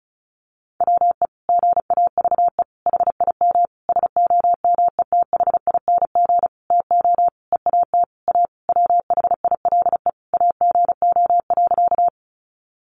Morse code, heard as a keyed tone, about 34 wpm, 700 Hz. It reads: WE GAVE HIM SOMETHING TO EAT A WHILE AGO.
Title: Sentences from Top 600 Words 35wpm